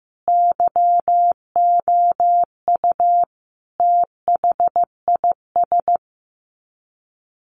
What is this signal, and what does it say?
Morse code, heard as a keyed tone, about 15 wpm, 700 Hz. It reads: YOU THIS
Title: Sets of 5 Words 15wpm